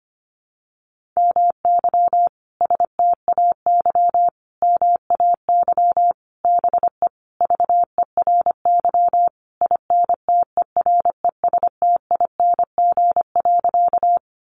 Morse code, heard as a keyed tone, about 25 wpm, 700 Hz. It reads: MY STAY MAY BE VERY INTERESTING.
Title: Sentences from Top 1000 Words 25wpm